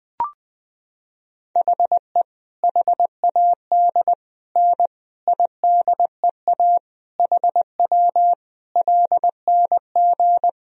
Morse code, heard as a keyed tone, about 20 wpm, 700 Hz. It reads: HE HAD N IDEA HW LNG
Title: Sentences from Top 300 Words and Abbreviations 20wpm